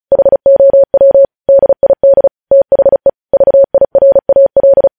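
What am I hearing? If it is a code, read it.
Morse code, heard as a keyed tone, about 36 wpm, 550 Hz. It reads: HOW DID THE VIRAL
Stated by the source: Morse code news headlines sent at 35.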